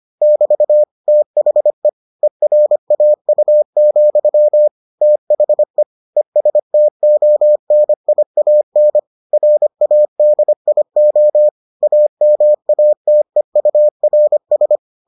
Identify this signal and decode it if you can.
Morse code, heard as a keyed tone, about 25 wpm, 600 Hz. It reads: <BT> THE ERAU, THE ESTONIAN RADIO AMATEURS